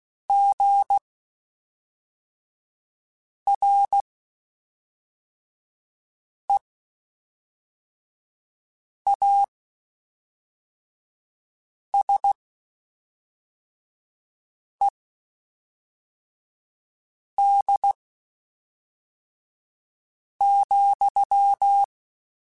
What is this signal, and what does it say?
Morse code, heard as a keyed tone, about 16 wpm, spaced as for 3 wpm, 800 Hz. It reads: GREASED,
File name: Spaced / Chuckles Spaced-mp3s / Squeaky sp @5